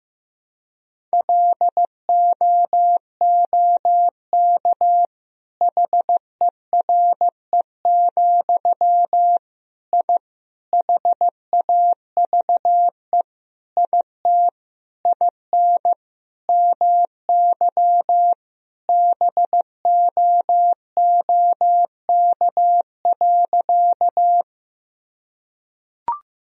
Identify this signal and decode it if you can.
Morse code, heard as a keyed tone, about 15 wpm, 700 Hz. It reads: LOOK HERE, I HAVE IT IN MY BOOK.
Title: Sentences from Top 1000 Words - Review Part 1 - 15wpm